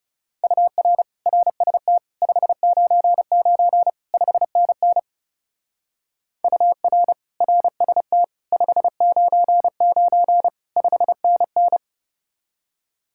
Morse code, heard as a keyed tone, about 36 wpm, 700 Hz. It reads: UR RST 599 5NN UR RST 599 5NN
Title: Speed-Racing Common QSO Phrases - 30wpm